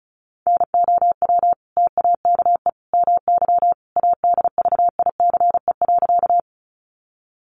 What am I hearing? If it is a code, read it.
Morse code, heard as a keyed tone, about 34 wpm, 700 Hz. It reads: NOW TAKE MY ADVICE.
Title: Sentences from Top 1100 Words 35wpm